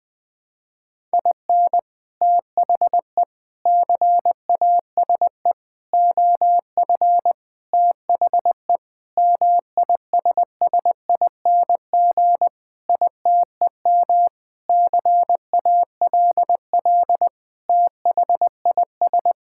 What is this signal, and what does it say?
Morse code, heard as a keyed tone, about 20 wpm, 700 Hz. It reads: IN THE CASE OF THE MISSING ITEM CALL THIS
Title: Sentences with 5-Word Phrases - 20wpm